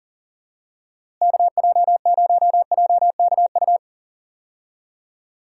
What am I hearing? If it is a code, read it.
Morse code, heard as a keyed tone, about 40 wpm, 700 Hz. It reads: KJ0JKU
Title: USA Call Signs - 6 Characters - Repeated 2x - 40wpm